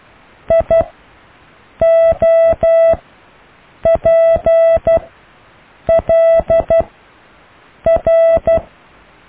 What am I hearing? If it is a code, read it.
Morse code, heard as a keyed tone, about 12 wpm, spaced as for 6 wpm, 660 Hz. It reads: IOPLR
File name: cw / IJLUM+CDRO+EKTPV+BFHNW